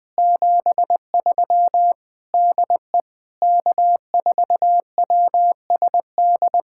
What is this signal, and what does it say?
Morse code, heard as a keyed tone, about 20 wpm, 700 Hz. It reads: 73 DE K4WSD
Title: POTA QSO Practice - 20wpm